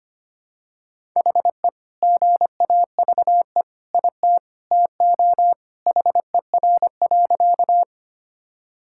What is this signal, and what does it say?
Morse code, heard as a keyed tone, about 25 wpm, 700 Hz. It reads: HE GAVE IT TO HER.
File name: practice / 25wpm / Sentences with 2-Word Phrases - 25wpm